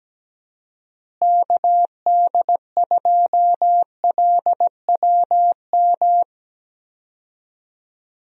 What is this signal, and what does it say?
Morse code, heard as a keyed tone, about 17 wpm, 700 Hz. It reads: KD2LWM
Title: USA Call Signs - 6 Characters - Repeated 3x - 17wpm